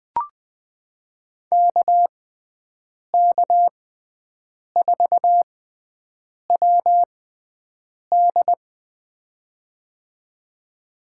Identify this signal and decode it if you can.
Morse code, heard as a keyed tone, about 20 wpm, spaced as for 7 wpm, 700 Hz. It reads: KK4WD
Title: USA Call Signs - 5 Characters - 6x Character Spacing - 20wpm